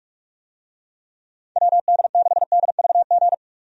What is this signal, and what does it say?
Morse code, heard as a keyed tone, about 46 wpm, 700 Hz. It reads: WD6DVG
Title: USA Call Signs - 6 Characters - 45wpm